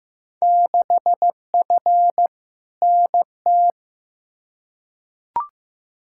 Morse code, heard as a keyed tone, about 15 wpm, 700 Hz. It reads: 6F NT
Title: Field Day Exchanges Encore - 15wpm